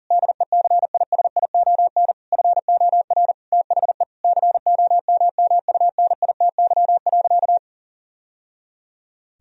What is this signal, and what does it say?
Morse code, heard as a keyed tone, about 37 wpm, 690 Hz. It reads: DECISION FOR THE COMMUNITY.
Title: Sentences from Top 1000 Words - Review Part 4 - 40wpm